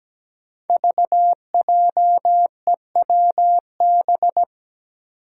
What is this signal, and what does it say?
Morse code, heard as a keyed tone, about 17 wpm, 700 Hz. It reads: VJEWB
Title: Five Challenging Letter-Number Combinations 17wpm